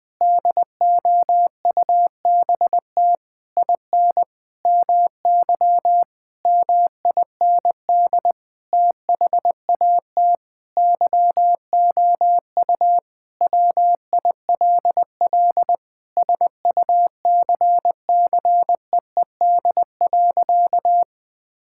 Morse code, heard as a keyed tone, about 21 wpm, 700 Hz. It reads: DOUBT IN MY MIND THAT YOU WILL SUCCEED.
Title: Sentences with 8-Word Phrases - 20wpm